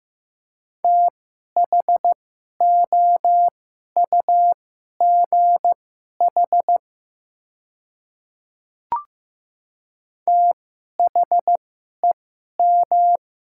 Morse code, heard as a keyed tone, about 15 wpm, spaced as for 11 wpm, 700 Hz. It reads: THOUGH THEM
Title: 2x Character Spacing - Top 500 Words - 15wpm